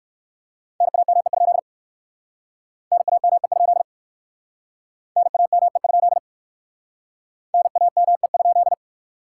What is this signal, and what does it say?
Morse code, heard as a keyed tone, about 63 wpm, 700 Hz. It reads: NAME? NAME? NAME? NAME?